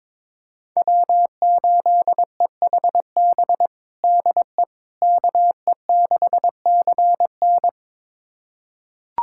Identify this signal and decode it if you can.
Morse code, heard as a keyed tone, about 22 wpm, 700 Hz. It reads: W8EHB DE KE6CN
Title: USA Callsign DE Callsign - 5 Characters - 22wpm